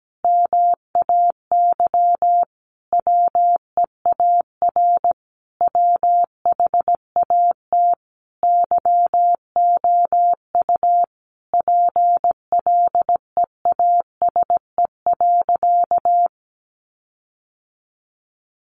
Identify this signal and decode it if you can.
Morse code, heard as a keyed tone, about 17 wpm, 700 Hz. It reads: MAY WEAR WHAT YOU PLEASE.
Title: Sentences from Top 1500 Words 17wpm